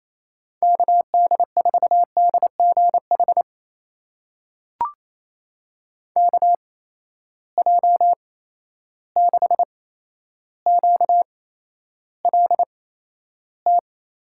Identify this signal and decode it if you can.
Morse code, heard as a keyed tone, about 28 wpm, spaced as for 12 wpm, 700 Hz. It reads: KD4DGH KJ6QLT
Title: USA Call Signs - 6 Characters - 8x Character Spacing - 28wpm